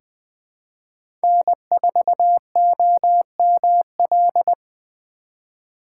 Morse code, heard as a keyed tone, about 20 wpm, 700 Hz. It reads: N4OML
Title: USA Call Signs - 5 Characters - 4x Character Spacing - 20wpm